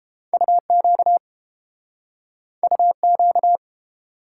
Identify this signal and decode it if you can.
Morse code, heard as a keyed tone, about 32 wpm, 700 Hz. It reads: UQ UQ